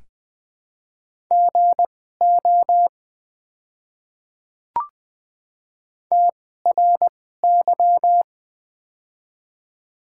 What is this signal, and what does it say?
Morse code, heard as a keyed tone, about 20 wpm, spaced as for 15 wpm, 700 Hz. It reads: GO TRY
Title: 2x Character Spacing - Top 200 Words - 20wpm